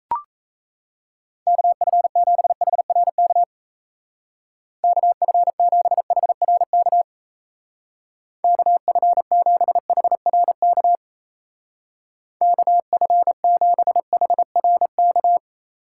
Morse code, heard as a keyed tone, about 34 wpm, 700 Hz. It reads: KF7HRK KF7HRK KF7HRK KF7HRK
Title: Speed Racing - USA 2x3 Call Signs 28wpm